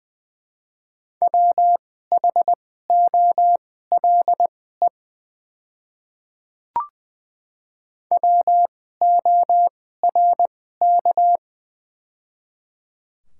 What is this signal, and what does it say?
Morse code, heard as a keyed tone, about 20 wpm, spaced as for 14 wpm, 700 Hz. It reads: WHOLE WORK
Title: 2x Character Spacing - Top 500 Words - 20wpm